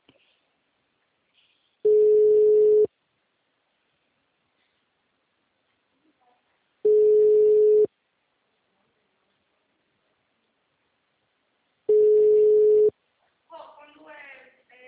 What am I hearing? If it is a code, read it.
Ringback tone (Europe, 425 Hz), with speech.
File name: out- -2014- - - .25131